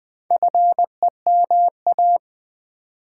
Morse code, heard as a keyed tone, about 20 wpm, 700 Hz. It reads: FEMA
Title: Build A Word - 2 Letters At A Time - 6-Letter Words - 20wpm